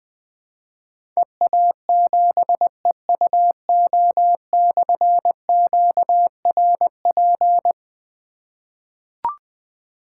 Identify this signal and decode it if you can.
Morse code, heard as a keyed tone, about 20 wpm, 700 Hz. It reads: EA7EUO/QRP
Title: Difficult Call Signs 20wpm